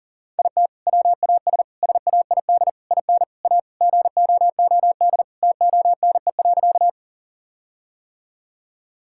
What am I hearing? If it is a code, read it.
Morse code, heard as a keyed tone, about 40 wpm, 700 Hz. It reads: IT WAS SAID IN A GOOD TONE.
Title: Sentences from Top 1600 Words 40wpm